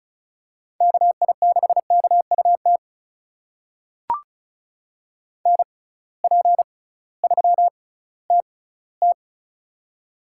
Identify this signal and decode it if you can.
Morse code, heard as a keyed tone, about 36 wpm, spaced as for 19 wpm, 700 Hz. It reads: KI6KUT NP3TT
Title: USA Call Signs - 6x Character Spacing - 35wpm